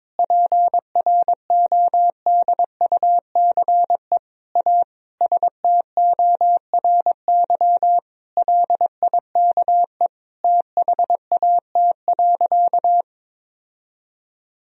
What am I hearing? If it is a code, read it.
Morse code, heard as a keyed tone, about 22 wpm, 700 Hz. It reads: PRODUCE A STORY LIKE THAT.
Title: Sentences from Top 1000 Words - Review Part 3 - 22wpm